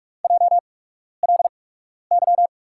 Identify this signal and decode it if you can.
Morse code, heard as a keyed tone, about 44 wpm, spaced as for 12 wpm, 700 Hz. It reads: JLY
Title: USA Call Signs - 8x Character Spacing - 45wpm